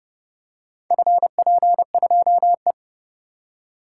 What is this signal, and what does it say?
Morse code, heard as a keyed tone, about 30 wpm, 700 Hz. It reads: FP2E